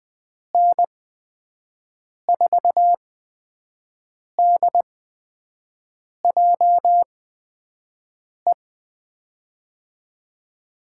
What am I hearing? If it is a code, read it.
Morse code, heard as a keyed tone, about 20 wpm, spaced as for 5 wpm, 700 Hz. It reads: N4DJE
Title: USA Call Signs - 6 Characters - 8x Character Spacing - 20wpm